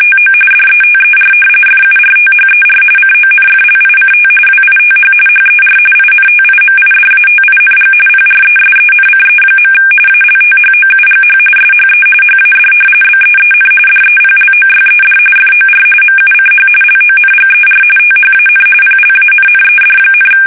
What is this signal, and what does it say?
RTTY, about 45.45 baud. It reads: TO DETER FURTHER CONFLICT WITH HER INTIMIDATING PRESENCE. ALTHOUGH SHE HAD BEEN DEPLOYED FOR MUCH LONGER THAN PLANNED,
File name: fsk850 / battleshipnj